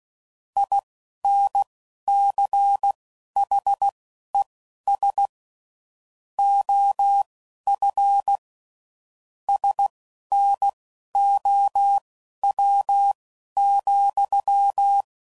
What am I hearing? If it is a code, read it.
Morse code, heard as a keyed tone, about 16 wpm, spaced as for 11 wpm, 800 Hz. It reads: INCHES OF SNOW,
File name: CW Code Groups @10WPM / QSOs / QSO WA7QOX